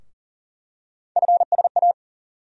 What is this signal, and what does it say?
Morse code, heard as a keyed tone, about 40 wpm, 700 Hz. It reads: FSA
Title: USA Call Sign Postfixes - 3 Letters 40wpm